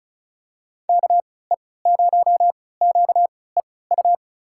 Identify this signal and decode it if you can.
Morse code, heard as a keyed tone, about 36 wpm, spaced as for 20 wpm, 700 Hz. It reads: KE0QEU
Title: USA Call Signs - 3x Character Spacing - 35wpm